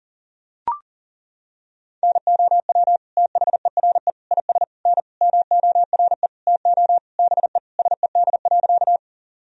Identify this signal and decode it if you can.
Morse code, heard as a keyed tone, about 40 wpm, 700 Hz. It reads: NOW THERE IS N MORE TO BE SED.
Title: Sentences from Top 100 Words and Abbreviations 40wpm